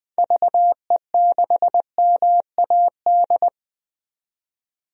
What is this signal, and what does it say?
Morse code, heard as a keyed tone, about 20 wpm, 700 Hz. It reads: VE6MAD
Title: DXCC-100 Call Signs - 6 Characters - 20wpm